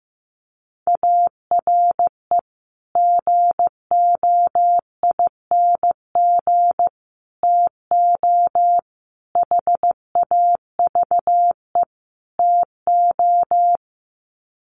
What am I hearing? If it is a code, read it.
Morse code, heard as a keyed tone, about 15 wpm, 700 Hz. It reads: ARE GOING TO HAVE TO
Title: Sets of 5 Words 15wpm